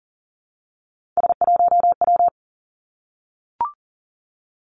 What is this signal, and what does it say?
Morse code, heard as a keyed tone, about 42 wpm, 700 Hz. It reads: S1W
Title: Three Challenging Letter-Number Combinations 40wpm